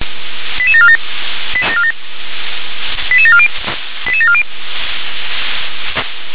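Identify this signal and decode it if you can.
Five-tone selective call (ZVEI), heard as quick runs of five tones, tones of about 70 ms. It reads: 88747 88740 88740
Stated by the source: ZVEI five-tone selective call 88631, received by SDR.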